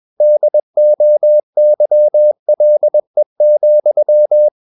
Morse code, heard as a keyed tone, about 21 wpm, 600 Hz. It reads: DOYLE,
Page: CW med Gnister